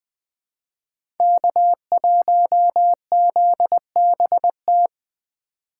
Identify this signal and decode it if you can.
Morse code, heard as a keyed tone, about 20 wpm, 700 Hz. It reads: K1ZBT
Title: USA Call Signs - 5 Characters - 4x Character Spacing - 20wpm